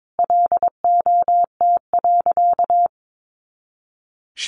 Morse code, heard as a keyed tone, about 22 wpm, 700 Hz. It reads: LOT.
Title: Additional Easy Phrases 22wpm